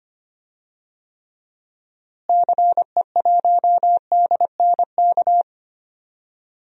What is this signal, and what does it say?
Morse code, heard as a keyed tone, about 25 wpm, 700 Hz. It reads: CE1DNK
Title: DXCC-100 Call Signs - 6 Characters - Repeated 3x - 25wpm